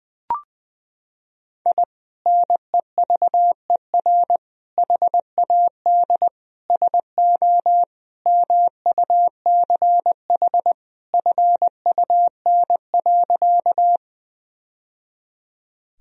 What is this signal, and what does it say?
Morse code, heard as a keyed tone, about 20 wpm, 700 Hz. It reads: I NEVER HAD SO MUCH FUN.
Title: Sentences from Top 1000 Words - Review Part 4 - 20wpm